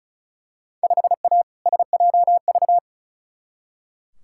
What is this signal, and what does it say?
Morse code, heard as a keyed tone, about 36 wpm, 700 Hz. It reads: 5A SJV
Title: Field Day Exchanges - 35wpm